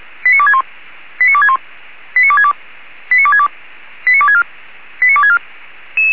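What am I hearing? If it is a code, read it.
Five-tone selective call (ZVEI), heard as quick runs of five tones, tones of about 70 ms. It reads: 89261 89261 89262 89262 89264 89264
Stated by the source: ZVEI five-tone call received by SDR.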